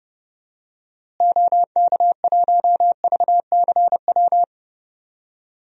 Morse code, heard as a keyed tone, about 32 wpm, 700 Hz. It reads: OK1VCW
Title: DXCC-100 Call Signs - 6 Characters - Repeated 3x - 30wpm